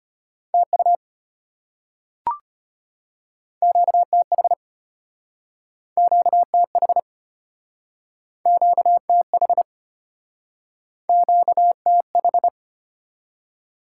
Morse code, heard as a keyed tone, about 34 wpm, 700 Hz. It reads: TU QTH QTH QTH QTH